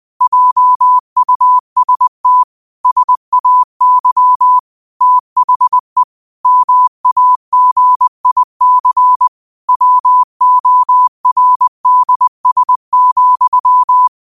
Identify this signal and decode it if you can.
Morse code, heard as a keyed tone, about 20 wpm, 1000 Hz. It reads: JUST SAY THE MAGIC WORDS,
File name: 20wpm